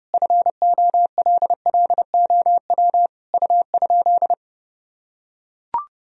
Morse code, heard as a keyed tone, about 30 wpm, 700 Hz. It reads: FOLLOW U?
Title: Sentences from Top 200 Words and Abbreviations 30wpm